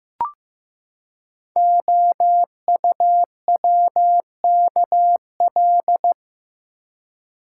Morse code, heard as a keyed tone, about 15 wpm, 700 Hz. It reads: OUWKL
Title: Code Groups - 5 Letters - 15wpm